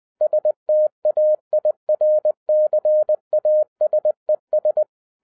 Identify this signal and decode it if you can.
Morse code, heard as a keyed tone, about 20 wpm, 610 Hz. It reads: STAIRCASES
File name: 2011-07-mp3 / n33 THE 3.11 2 20wpm